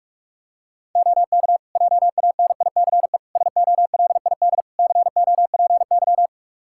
Morse code, heard as a keyed tone, about 46 wpm, 700 Hz. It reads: OK JANICE SOLID COPY
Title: Practice QSO - Exchange 3 - More Info - 45wpm